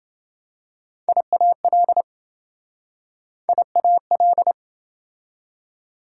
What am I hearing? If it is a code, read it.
Morse code, heard as a keyed tone, about 27 wpm, 700 Hz. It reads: IAL IAL